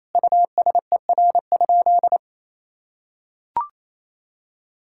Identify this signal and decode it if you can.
Morse code, heard as a keyed tone, about 28 wpm, 700 Hz. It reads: USER?
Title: Sentences from Top 1000 Words - Review Part 5 - 28wpm